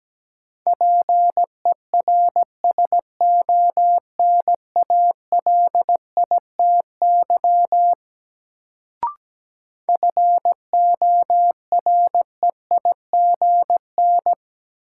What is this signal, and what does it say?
Morse code, heard as a keyed tone, about 17 wpm, 700 Hz. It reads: PERSONALITY FOREIGN
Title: Sentences from Top 1400 Words - New Word Drill Alt - 17wpm